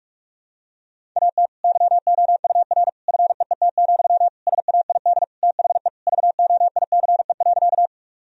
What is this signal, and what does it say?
Morse code, heard as a keyed tone, about 46 wpm, 700 Hz. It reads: AT YOUR FEET, SAID THE VOICE.
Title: Sentences from Top 1000 Words - Review Part 1 - 45wpm